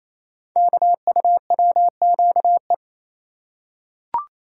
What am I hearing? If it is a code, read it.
Morse code, heard as a keyed tone, about 27 wpm, 700 Hz. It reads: KUWQE
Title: Code Groups - 5 Letters - 28wpm